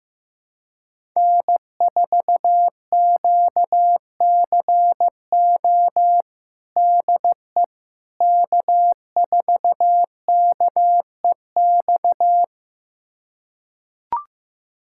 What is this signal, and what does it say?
Morse code, heard as a keyed tone, about 15 wpm, 700 Hz. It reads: N4QCO DE K4KEX
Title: USA Callsign DE Callsign 15wpm